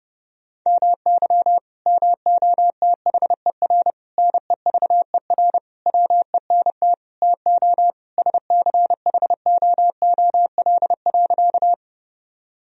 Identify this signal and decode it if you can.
Morse code, heard as a keyed tone, about 30 wpm, 700 Hz. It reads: MY MOTHER NEVER WENT TO SCHOOL.
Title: Sentences from Top 1000 Words - Review Part 1 - 30wpm